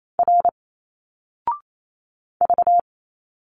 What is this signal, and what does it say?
Morse code, heard as a keyed tone, about 28 wpm, 700 Hz. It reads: R V
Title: Single Letters 28wpm